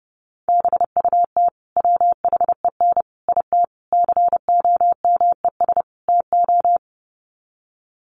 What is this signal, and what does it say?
Morse code, heard as a keyed tone, about 30 wpm, 700 Hz. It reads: BUT WHEN IT COMES TO
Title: Sets of 5 Words 30wpm